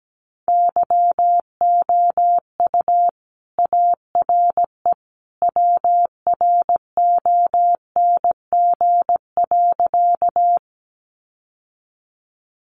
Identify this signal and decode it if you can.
Morse code, heard as a keyed tone, about 17 wpm, 700 Hz. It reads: YOU ARE WRONG.
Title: Sentences from Top 900 Words 17wpm